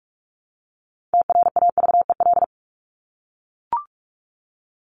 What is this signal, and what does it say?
Morse code, heard as a keyed tone, about 44 wpm, 700 Hz. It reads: TRAVEL
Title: Top 500 Words 45wpm